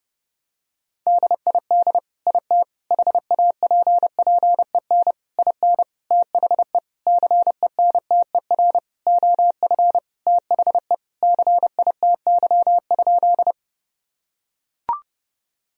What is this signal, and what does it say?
Morse code, heard as a keyed tone, about 30 wpm, 700 Hz. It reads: DID IT HAPPEN IN THE CENTER OF THE CITY?
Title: Sentences from Top 1000 Words - Review Part 2 - 30wpm